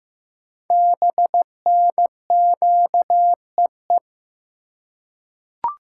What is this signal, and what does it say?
Morse code, heard as a keyed tone, about 15 wpm, 700 Hz. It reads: BNQEE
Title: Code Groups - 5 Letters - 15wpm